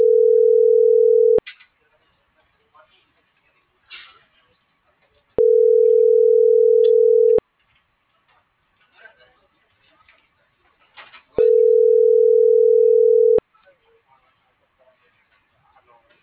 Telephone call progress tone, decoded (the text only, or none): ringback tone